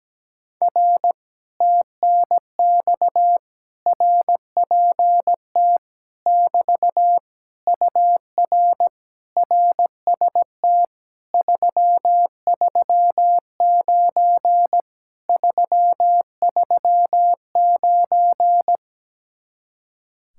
Morse code, heard as a keyed tone, about 17 wpm, 700 Hz. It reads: R TNX RPT <BT> UR RST 339 339
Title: Speed-Racing Common QSO Phrases - 17wpm